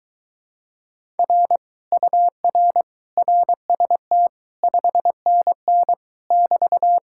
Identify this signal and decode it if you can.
Morse code, heard as a keyed tone, about 23 wpm, 700 Hz. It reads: R UR RST 5NN <BT>